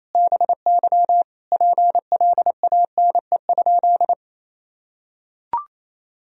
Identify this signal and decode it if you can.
Morse code, heard as a keyed tone, about 27 wpm, 700 Hz. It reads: BY PLANE?
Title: Sentences from Top 1000 Words - Review Part 3 - 28wpm